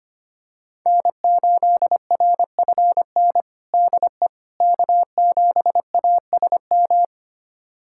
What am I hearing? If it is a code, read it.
Morse code, heard as a keyed tone, about 25 wpm, 700 Hz. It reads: N8RFN DE K7ASM
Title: USA Callsign DE Callsign - 5 Characters - 25wpm